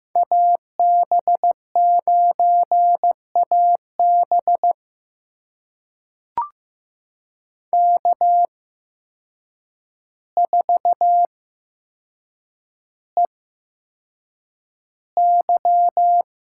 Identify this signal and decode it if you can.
Morse code, heard as a keyed tone, about 15 wpm, spaced as for 7 wpm, 700 Hz. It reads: AB9AB K4EY
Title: USA Call Signs - 5 Characters - 8x Character Spacing - 15wpm